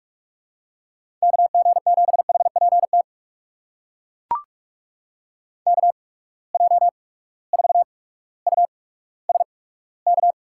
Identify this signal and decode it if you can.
Morse code, heard as a keyed tone, about 42 wpm, spaced as for 19 wpm, 700 Hz. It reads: KG7HPT KJ4USK
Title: USA Call Signs - 6 Characters - 8x Character Spacing - 45wpm